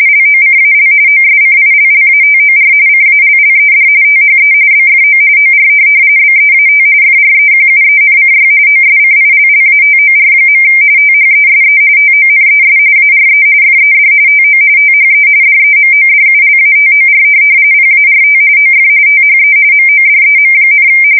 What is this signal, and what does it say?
RTTY, about 45.45 baud. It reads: LOCAL OSCILLATOR CALLED THE BFO OR BEAT FREQUENCY OSCILLATOR. THESE TONES ARE FED TO THE DEMODULATOR PART OF THE MODEM,